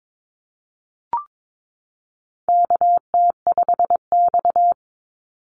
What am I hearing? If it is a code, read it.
Morse code, heard as a keyed tone, about 22 wpm, 700 Hz. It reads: KT5X
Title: Call Signs - 22wpm